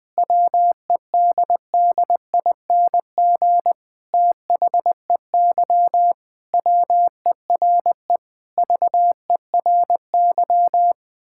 Morse code, heard as a keyed tone, about 20 wpm, 700 Hz. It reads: WEDDING THEY WERE VERY
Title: Sentences with 6-Word Phrases - Spoken Beginning - 20wpm